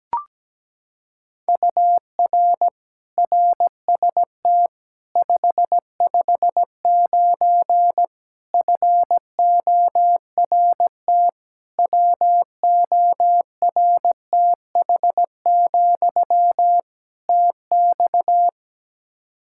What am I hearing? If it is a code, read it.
Morse code, heard as a keyed tone, about 17 wpm, 700 Hz. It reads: UR RST 559 FORT WORTH, TX
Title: Practice QSO - Exchange 1 - Beginning - 17wpm